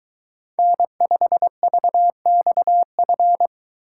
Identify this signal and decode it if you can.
Morse code, heard as a keyed tone, about 22 wpm, 700 Hz. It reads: N5VXF